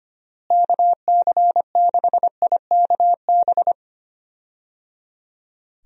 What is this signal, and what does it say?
Morse code, heard as a keyed tone, about 25 wpm, 700 Hz. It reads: KC6IKB
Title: USA Call Signs - 6 Characters - Repeated 3x - 25wpm